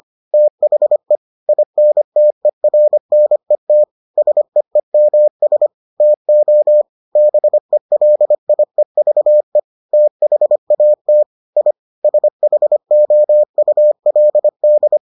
Morse code, heard as a keyed tone, about 25 wpm, 600 Hz. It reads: THE INTERNET SEEMS TO BELIEVE THAT I SHOULD